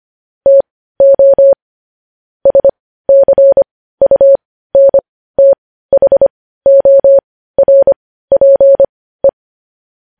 Morse code, heard as a keyed tone, about 25 wpm, spaced as for 15 wpm, 550 Hz. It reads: TO SCUNTHORPE
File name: podcast / News headlines-1x-25x15